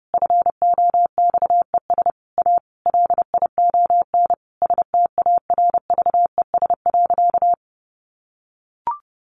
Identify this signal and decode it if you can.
Morse code, heard as a keyed tone, about 30 wpm, 700 Hz. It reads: FOXES A LION STARVES.
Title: A Sound Of Thunder 30wpm